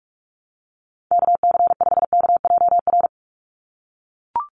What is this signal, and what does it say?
Morse code, heard as a keyed tone, about 44 wpm, 700 Hz. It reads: KC5KJR
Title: USA Call Signs 45wpm